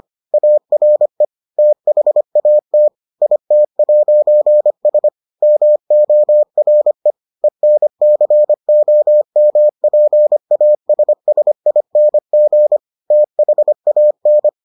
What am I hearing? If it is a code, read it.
Morse code, heard as a keyed tone, about 25 wpm, 600 Hz. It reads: ARE THAT IT'S MORE ENCOMPASSING THAN